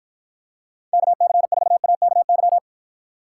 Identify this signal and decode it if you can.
Morse code, heard as a keyed tone, about 53 wpm, 700 Hz. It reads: KC4AKY